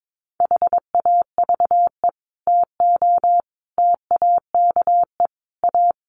Morse code, heard as a keyed tone, about 22 wpm, 700 Hz. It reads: HAVE TO TAKE A
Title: Sets of 5 Words Encore 22wpm